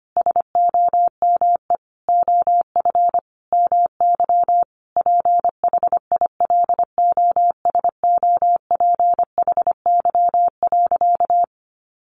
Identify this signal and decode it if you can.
Morse code, heard as a keyed tone, about 25 wpm, 700 Hz. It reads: SOME OF MY PHILOSOPHY.
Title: Sentences from Top 900 Words 25wpm